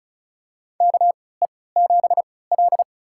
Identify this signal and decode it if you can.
Morse code, heard as a keyed tone, about 36 wpm, spaced as for 20 wpm, 700 Hz. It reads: KE7L
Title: USA Call Signs - 3x Character Spacing - 35wpm